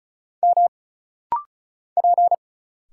Morse code, heard as a keyed tone, about 34 wpm, 700 Hz. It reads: M P
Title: Single Letters - Rapid-Fire 35wpm